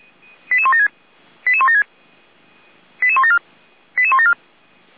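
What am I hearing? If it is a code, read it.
Five-tone selective call (ZVEI), heard as quick runs of five tones, tones of about 70 ms. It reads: 88176 88176 88174 88174
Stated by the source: ZVEI five-tone call received by SDR, recorded by fire brigade Feuerwehr Orlamünde.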